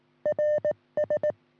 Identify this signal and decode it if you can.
Morse code, heard as a keyed tone, about 18 wpm, 590 Hz. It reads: RS
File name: sm7lcb / 1941 sm0err 10g rs